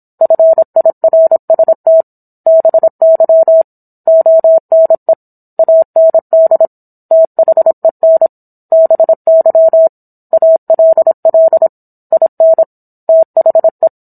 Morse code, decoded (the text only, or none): FIRST BY ONE AND THEN BY ALL IN THE